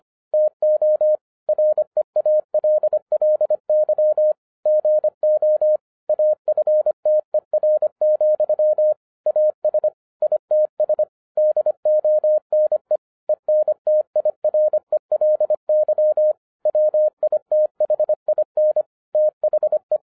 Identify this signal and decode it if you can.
Morse code, heard as a keyed tone, about 25 wpm, 610 Hz. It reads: TO REALLY GO AFTER, AS ITS DONE ENTIRELY WITHIN THE
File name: 2012-01-mp3 / n14-FUN WITH NUMBERS-25wpm